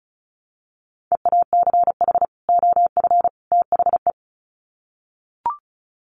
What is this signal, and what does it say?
Morse code, heard as a keyed tone, about 34 wpm, 700 Hz. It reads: EACH OF THE
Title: Sets of 3 Words 35wpm